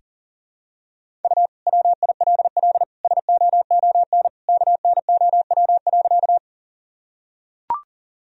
Morse code, heard as a keyed tone, about 40 wpm, 700 Hz. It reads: U WILL SOON KNOW.
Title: Sentences from Top 400 Words and Abbreviations 40wpm